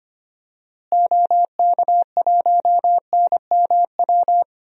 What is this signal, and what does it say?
Morse code, heard as a keyed tone, about 25 wpm, 690 Hz. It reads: OK1NMW
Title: DXCC-100 Call Signs - 6 Characters - Repeated 3x - 25wpm